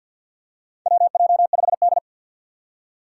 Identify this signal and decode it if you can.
Morse code, heard as a keyed tone, about 48 wpm, 700 Hz. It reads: WJ5D
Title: USA Call Signs - 4 Characters - 6x Character Spacing - 50wpm